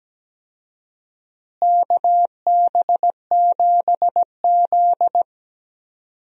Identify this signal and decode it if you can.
Morse code, heard as a keyed tone, about 17 wpm, 700 Hz. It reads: KB7Z
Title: USA Call Signs - 4 Characters - Repeated 2x - 17wpm